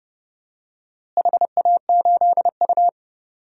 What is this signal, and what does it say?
Morse code, heard as a keyed tone, about 30 wpm, 700 Hz. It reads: HA8U
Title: DXCC-100 Call Signs - 4 Characters - Repeated 3x - 30wpm